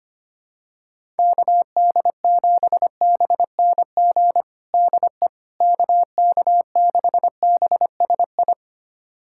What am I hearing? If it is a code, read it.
Morse code, heard as a keyed tone, about 25 wpm, 700 Hz. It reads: KD7BNG DE KK6BSI
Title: USA Callsign DE Callsign - 6 Characters - 25wpm